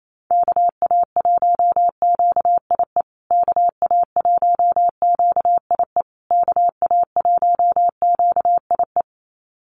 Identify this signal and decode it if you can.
Morse code, heard as a keyed tone, about 28 wpm, 700 Hz. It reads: KA1QIE KA1QIE KA1QIE
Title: USA Call Signs Repeated 3x 28wpm